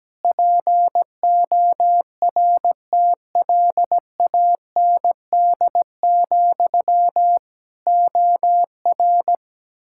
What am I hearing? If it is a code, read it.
Morse code, heard as a keyed tone, about 17 wpm, 700 Hz. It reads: PORTLAND, OR
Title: K1USN SST QSO Practice 17wpm